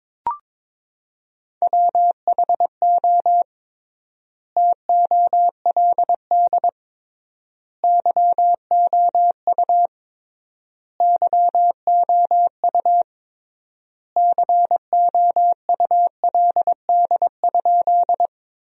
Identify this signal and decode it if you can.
Morse code, heard as a keyed tone, about 22 wpm, 700 Hz. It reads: WHO TOLD YOU YOU COULD?
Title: 3x Word-Spacing - Sentences from Top 400 Words - 22wpm